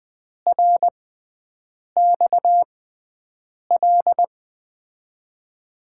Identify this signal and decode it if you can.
Morse code, heard as a keyed tone, about 20 wpm, spaced as for 7 wpm, 700 Hz. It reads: RXL
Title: USA Call Signs - 6 Characters - 6x Character Spacing - 20wpm